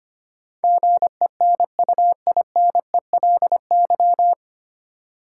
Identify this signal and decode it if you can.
Morse code, heard as a keyed tone, about 25 wpm, 700 Hz. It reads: GENUINELY
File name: Nine Letter Words - 25wpm